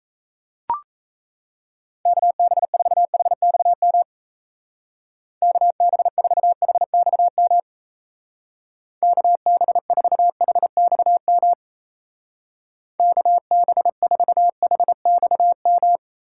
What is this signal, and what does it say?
Morse code, heard as a keyed tone, about 33 wpm, 700 Hz. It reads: KB4HXM KB4HXM KB4HXM KB4HXM
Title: Speed Racing - USA 2x3 Call Signs 28wpm